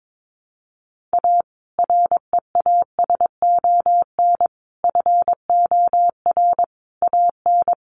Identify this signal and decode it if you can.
Morse code, heard as a keyed tone, about 22 wpm, 700 Hz. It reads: A REASON FOR AN
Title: Sentences from Top 1300 Words 22wpm